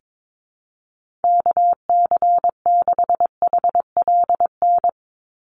Morse code, heard as a keyed tone, about 22 wpm, 700 Hz. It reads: KC6HLN
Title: USA Call Signs 22wpm